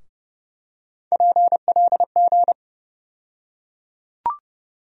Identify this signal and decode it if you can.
Morse code, heard as a keyed tone, about 30 wpm, 700 Hz. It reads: PLG
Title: USA Call Sign Postfixes - 3 Letters 30wpm